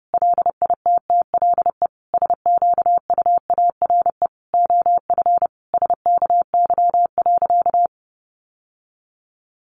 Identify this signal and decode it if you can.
Morse code, heard as a keyed tone, about 28 wpm, 700 Hz. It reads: LITTLE SQUARE OF SKY.
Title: Sentences from Top 1200 Words 30wpm